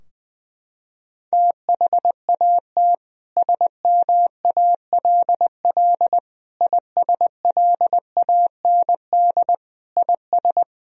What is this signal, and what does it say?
Morse code, heard as a keyed tone, about 21 wpm, 700 Hz. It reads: THAT SMALL ISLAND IS
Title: Sentences from Top 1000 Words - Review Part 4 - 20wpm